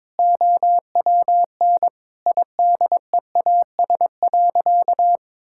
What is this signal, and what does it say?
Morse code, heard as a keyed tone, about 22 wpm, 700 Hz. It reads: OWN IDEAS.
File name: practice / 22wpm / Sentences with 2-Word Phrases - 22wpm